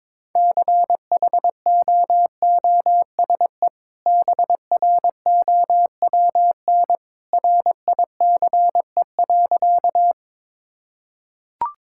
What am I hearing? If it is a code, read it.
Morse code, heard as a keyed tone, about 22 wpm, 700 Hz. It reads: CHOOSE BROWN RICE.
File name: practice / 22wpm / Easy Sentences with Hints - 22wpm